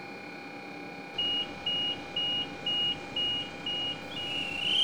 There is no code.